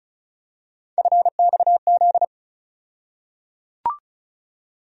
Morse code, heard as a keyed tone, about 36 wpm, 700 Hz. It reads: FXZ
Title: USA Call Sign Postfixes - 3 Letters 35wpm